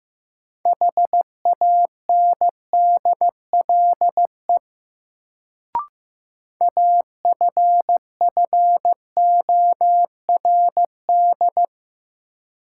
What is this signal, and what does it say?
Morse code, heard as a keyed tone, about 15 wpm, 700 Hz. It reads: HANDLE AFFORD
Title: Sentences from Top 1100 Words - New Word Drill - 15wpm